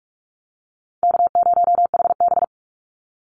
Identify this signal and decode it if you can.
Morse code, heard as a keyed tone, about 44 wpm, 700 Hz. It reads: K0HB
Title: Call Signs Repeated 2x - 45wpm